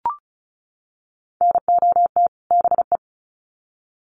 Morse code, heard as a keyed tone, about 34 wpm, 700 Hz. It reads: NOT BE
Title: Sets of 2 Words 35wpm